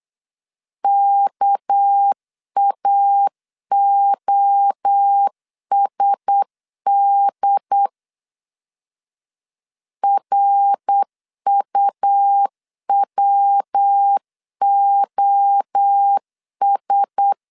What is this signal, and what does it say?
Morse code, heard as a keyed tone, about 8 wpm, 800 Hz. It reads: KAOSD RUWOS
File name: LEZIONI / 6 PKX0 / 6 PK 8wpm 3003